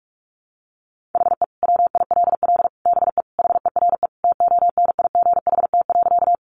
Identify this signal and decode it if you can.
Morse code, heard as a keyed tone, about 44 wpm, 700 Hz. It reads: HE WILL BE HERE TONIGHT.
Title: Sentences from Top 600 Words 45wpm